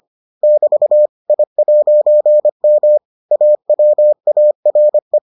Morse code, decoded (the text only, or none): <BT> I'M AWARE